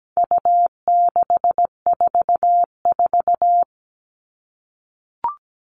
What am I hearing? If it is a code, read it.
Morse code, heard as a keyed tone, about 17 wpm, 700 Hz. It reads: U644
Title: All Cut Numbers - 4-Digit 17wpm